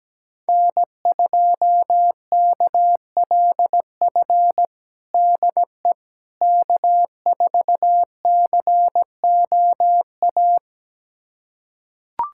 Morse code, heard as a keyed tone, about 17 wpm, 700 Hz. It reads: N2KLF DE K4COA
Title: USA Callsign DE Callsign - 5 Characters - 17wpm